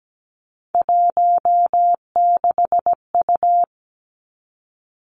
Morse code, heard as a keyed tone, about 17 wpm, 700 Hz. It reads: 16U
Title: Three Challenging Letter-Number Combinations 17wpm